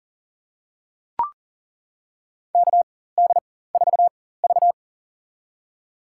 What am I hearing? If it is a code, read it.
Morse code, heard as a keyed tone, about 40 wpm, spaced as for 19 wpm, 700 Hz. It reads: KD4V
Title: USA Call Signs - 4 Characters - 4x Character Spacing - 40wpm